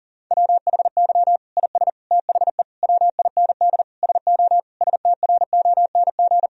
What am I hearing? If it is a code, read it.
Morse code, heard as a keyed tone, about 37 wpm, 700 Hz. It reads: WHY IS THE WIND SO STRONG
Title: Sentences from Top 1000 Words - Review Part 2 - 40wpm